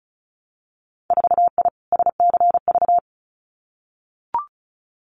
Morse code, heard as a keyed tone, about 36 wpm, 700 Hz. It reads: 4I SCV
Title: Winter Field Day Exchanges Encore 35wpm